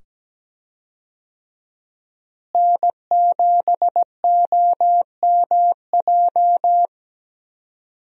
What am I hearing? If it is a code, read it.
Morse code, heard as a keyed tone, about 17 wpm, 700 Hz. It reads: N7OMJ
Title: USA Call Signs - 5 Characters - Repeated 2x - 17wpm